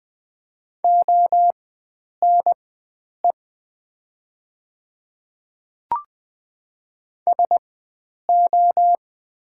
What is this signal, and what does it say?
Morse code, heard as a keyed tone, about 20 wpm, spaced as for 9 wpm, 690 Hz. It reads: ONE SO
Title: 4x Character Spacing - Top 100 Words - 20wpm